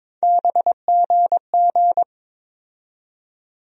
Morse code, heard as a keyed tone, about 22 wpm, 700 Hz. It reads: BGG
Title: USA Call Sign Postfixes - 3 Letters 22wpm